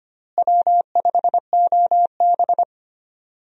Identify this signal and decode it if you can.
Morse code, heard as a keyed tone, about 25 wpm, 700 Hz. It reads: W5OB
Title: USA Call Signs - 4 Characters - Repeated 2x - 25wpm